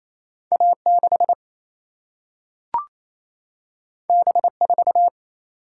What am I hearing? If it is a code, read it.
Morse code, heard as a keyed tone, about 28 wpm, 700 Hz. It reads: A 6 B 4
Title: Single-Letter Single-Number 28wpm